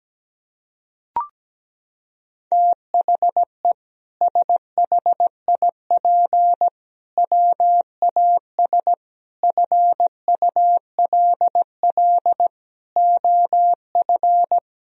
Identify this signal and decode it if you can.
Morse code, heard as a keyed tone, about 17 wpm, 700 Hz. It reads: THE SHIP WAS FULL OF
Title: Sentences from Top 1000 Words - Review Part 3 - 17wpm